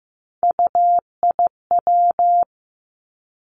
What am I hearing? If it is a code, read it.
Morse code, heard as a keyed tone, about 15 wpm, 700 Hz. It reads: UIW
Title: Three Challenging Letter-Number Combinations 15wpm